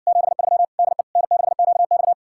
Morse code, heard as a keyed tone, about 60 wpm, 700 Hz. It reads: 73 DE N6QX